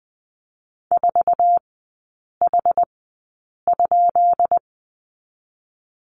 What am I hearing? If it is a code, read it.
Morse code, heard as a keyed tone, about 20 wpm, spaced as for 8 wpm, 690 Hz. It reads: 4H?
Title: 145 Lesson 24e - Letter y - Three Characters 20wpm